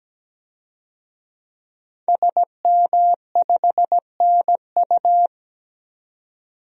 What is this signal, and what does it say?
Morse code, heard as a keyed tone, about 17 wpm, 700 Hz. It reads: SM5NU
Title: DXCC-100 Call Signs - 5 Characters - 17wpm